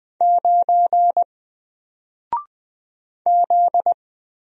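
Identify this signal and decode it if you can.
Morse code, heard as a keyed tone, about 20 wpm, 700 Hz. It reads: 9 Z
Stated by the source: Single letter and number Morse drill at 20 wpm, from N4LCD.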